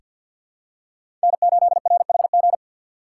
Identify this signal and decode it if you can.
Morse code, heard as a keyed tone, about 46 wpm, 700 Hz. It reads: N8RHG
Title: USA Call Signs - 5 Characters - 4x Character Spacing - 50wpm